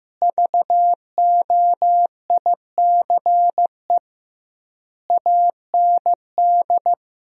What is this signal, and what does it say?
Morse code, heard as a keyed tone, about 15 wpm, 700 Hz. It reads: VOICE AND
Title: 2x Word-Spacing - Sentences from Top 500 Words - 15wpm